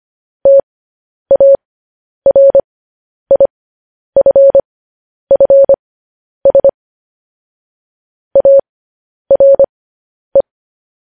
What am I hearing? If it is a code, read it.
Morse code, heard as a keyed tone, about 25 wpm, spaced as for 10 wpm, 550 Hz. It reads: TARIFFS ARE
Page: news headlines in morse code at 10 wpm